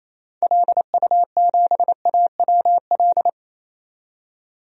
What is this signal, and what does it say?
Morse code, heard as a keyed tone, about 28 wpm, 700 Hz. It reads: LU7AWL
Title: DXCC-100 Call Signs - 6 Characters - 28wpm